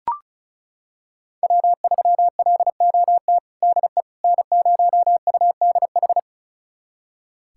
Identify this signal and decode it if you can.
Morse code, heard as a keyed tone, about 36 wpm, 690 Hz. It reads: W3LOT DE N0UDH
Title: USA Callsign DE Callsign - 5 Characters - 35wpm